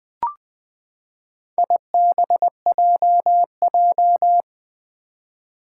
Morse code, heard as a keyed tone, about 20 wpm, 700 Hz. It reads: IBJJ
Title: Four Challenging Letter-Number Combinations 20wpm